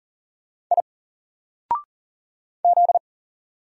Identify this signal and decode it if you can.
Morse code, heard as a keyed tone, about 40 wpm, 700 Hz. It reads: I Z
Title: Single Letter-Number 40wpm